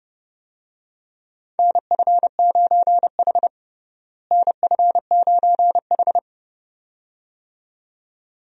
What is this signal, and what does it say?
Morse code, heard as a keyed tone, about 30 wpm, 700 Hz. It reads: NF9H NF9H
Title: USA Call Signs - 4 Characters - Repeated 2x - 30wpm